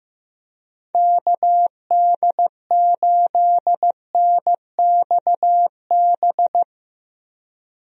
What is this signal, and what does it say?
Morse code, heard as a keyed tone, about 16 wpm, 700 Hz. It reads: KD8NXB
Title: USA Call Signs - 6 Characters - Repeated 3x - 15wpm